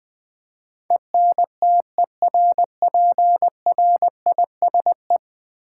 Morse code, heard as a keyed tone, about 20 wpm, 700 Hz. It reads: ENTERPRISE
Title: Ten Letter Words - 20wpm